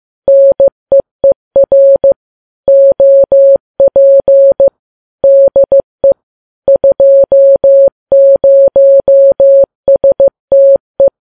Morse code, heard as a keyed tone, about 15 wpm, 550 Hz. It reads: NEER OP DE 20STE
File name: Files Web tls / Morse on9fv / morse15kps